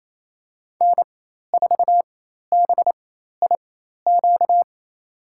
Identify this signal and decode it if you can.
Morse code, heard as a keyed tone, about 28 wpm, spaced as for 13 wpm, 700 Hz. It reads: N4BIQ
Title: USA Call Signs - 5 Characters - 4x Character Spacing - 28wpm